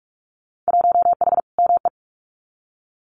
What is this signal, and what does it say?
Morse code, heard as a keyed tone, about 44 wpm, 700 Hz. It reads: 1H ME
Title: Winter Field Day Exchanges 45wpm